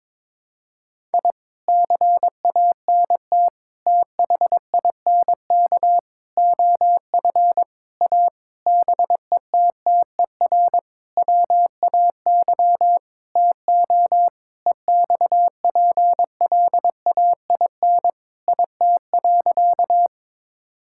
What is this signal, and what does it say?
Morse code, heard as a keyed tone, about 22 wpm, 700 Hz. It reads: I CANT THINK OF A BETTER WAY TO EXPLAIN IT.
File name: practice / 22wpm / Sentences with 8-Word Phrases - 22wpm